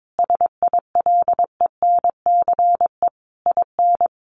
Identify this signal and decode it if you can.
Morse code, heard as a keyed tone, about 22 wpm, 700 Hz. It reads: SILENCE IN
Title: A Sound Of Thunder 22wpm